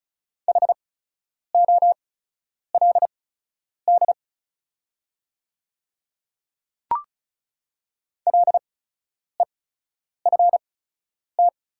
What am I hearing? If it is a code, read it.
Morse code, heard as a keyed tone, about 36 wpm, spaced as for 10 wpm, 700 Hz. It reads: HOLD LEFT
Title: 8x Character Spacing - Top 500 Words - 35wpm